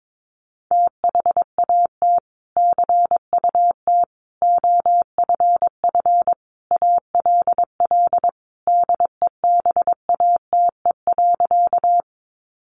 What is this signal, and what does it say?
Morse code, heard as a keyed tone, about 22 wpm, 700 Hz. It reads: THAT CUT OFF ALL DEBATE.
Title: Sentences from Top 1100 Words 22wpm